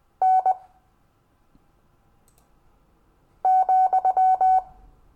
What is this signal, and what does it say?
Morse code, heard as a keyed tone, about 20 wpm, 750 Hz. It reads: N ,
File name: cw / Lesson15 five forwardslash